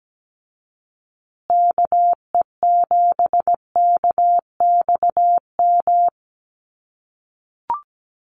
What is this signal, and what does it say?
Morse code, heard as a keyed tone, about 17 wpm, 700 Hz. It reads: KE7KXM
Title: USA Call Signs Repeated 2x 17wpm